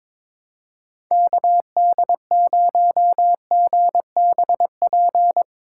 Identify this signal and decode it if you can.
Morse code, heard as a keyed tone, about 22 wpm, 700 Hz. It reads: KD0GBP
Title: USA Call Signs - 6 Characters - 8x Character Spacing - 22wpm